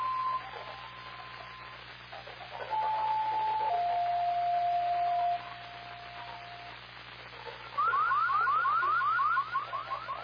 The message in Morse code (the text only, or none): none